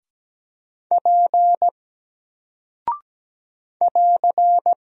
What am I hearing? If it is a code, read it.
Morse code, heard as a keyed tone, about 17 wpm, 700 Hz. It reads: P <AR>
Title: Single Character (Letter:Number:Punctuation:Prosign) Alt 17wpm